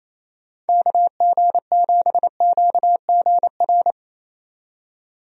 Morse code, heard as a keyed tone, about 28 wpm, 700 Hz. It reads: KG7QGR
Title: USA Call Signs - 6 Characters - 28wpm